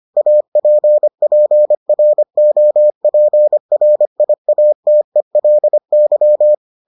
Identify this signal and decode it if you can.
Morse code, heard as a keyed tone, about 25 wpm, 600 Hz. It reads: APPROPRIATELY